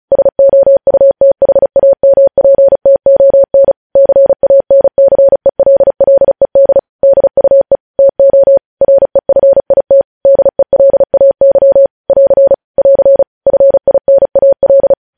Morse code, heard as a keyed tone, about 36 wpm, 550 Hz. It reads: SOUTHAMPTON CANCELLED DUE TO REFIT DELAY <AR> <AR> FINAL
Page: news headlines in morse code at 35 wpm